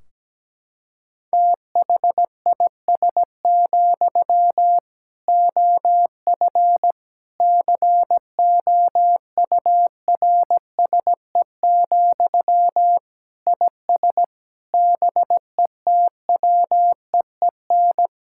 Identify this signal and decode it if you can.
Morse code, heard as a keyed tone, about 17 wpm, 700 Hz. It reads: THIS, OF COURSE, IS BETWEEN
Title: Sentences from Top 1600 Words 17wpm